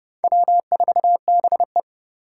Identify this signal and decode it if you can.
Morse code, heard as a keyed tone, about 37 wpm, 700 Hz. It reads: W4BE
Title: Speed Racing - USA 1x2 Call Signs 30wpm